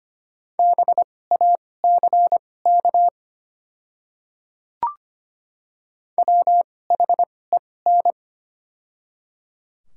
Morse code, heard as a keyed tone, about 25 wpm, spaced as for 18 wpm, 700 Hz. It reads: BACK WHEN
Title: 2x Character Spacing - Top 300 Words - 25wpm